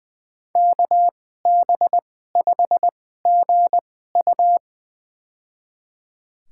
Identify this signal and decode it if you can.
Morse code, heard as a keyed tone, about 20 wpm, spaced as for 15 wpm, 700 Hz. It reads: KB5GU
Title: USA Call Signs - 2x Character Spacing - 20wpm